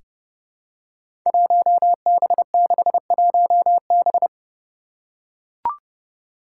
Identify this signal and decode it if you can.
Morse code, heard as a keyed tone, about 30 wpm, 700 Hz. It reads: 1B61B
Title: Five Challenging Letter-Number Combinations 30wpm